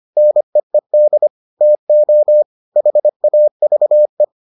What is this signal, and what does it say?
Morse code, heard as a keyed tone, about 25 wpm, 600 Hz. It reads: NEED TO HAVE